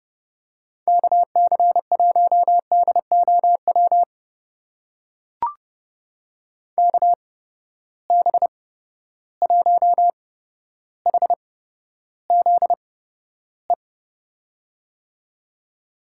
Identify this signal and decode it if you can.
Morse code, heard as a keyed tone, about 32 wpm, spaced as for 13 wpm, 700 Hz. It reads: KC1DOW KB1HZE
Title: USA Call Signs - 6 Characters - 8x Character Spacing - 30wpm